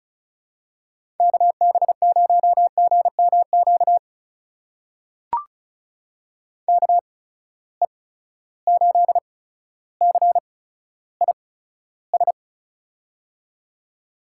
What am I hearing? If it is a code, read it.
Morse code, heard as a keyed tone, about 36 wpm, spaced as for 15 wpm, 700 Hz. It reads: KB0GMQ KE8CIS
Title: USA Call Signs - 6 Characters - 8x Character Spacing - 35wpm